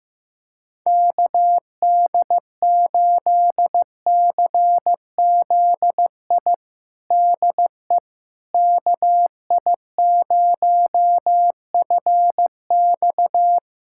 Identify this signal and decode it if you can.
Morse code, heard as a keyed tone, about 15 wpm, 700 Hz. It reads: KD8CZI DE KI0FX
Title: USA Callsign DE Callsign 15wpm